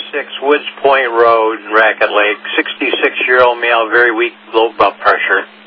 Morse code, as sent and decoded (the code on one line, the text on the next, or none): none